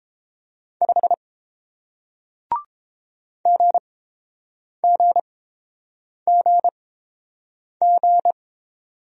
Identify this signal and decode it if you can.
Morse code, heard as a keyed tone, about 30 wpm, spaced as for 7 wpm, 700 Hz. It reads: 5 GGGG